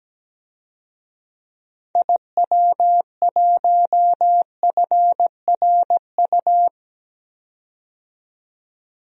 Morse code, heard as a keyed tone, about 17 wpm, 700 Hz. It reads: IW1FRU
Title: DXCC-100 Call Signs - 6 Characters - 17wpm